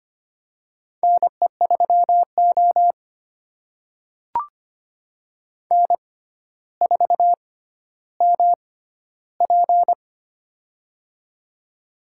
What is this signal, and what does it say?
Morse code, heard as a keyed tone, about 25 wpm, spaced as for 13 wpm, 700 Hz. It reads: NE3O N4MP
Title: USA Call Signs - 4 Characters - 6x Character Spacing - 25wpm